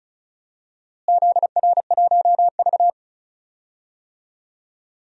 Morse code, heard as a keyed tone, about 34 wpm, 700 Hz. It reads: ZR1V
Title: DXCC-100 Call Signs - 4 Characters - Repeated 3x - 35wpm